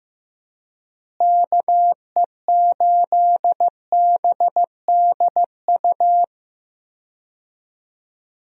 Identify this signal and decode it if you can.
Morse code, heard as a keyed tone, about 15 wpm, 700 Hz. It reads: KE8BDU
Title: USA Call Signs - 6 Characters - Repeated 3x - 15wpm